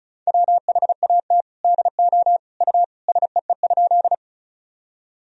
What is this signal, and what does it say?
Morse code, heard as a keyed tone, about 34 wpm, 700 Hz. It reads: WHAT DO U SEE?
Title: Sentences from Top 100 Words and Abbreviations 35wpm